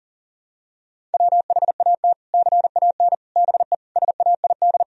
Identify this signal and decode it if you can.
Morse code, heard as a keyed tone, about 40 wpm, 700 Hz. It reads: WHAT CAN BE SAID
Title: Sentences from Top 1000 Words - Review Part 4 - 40wpm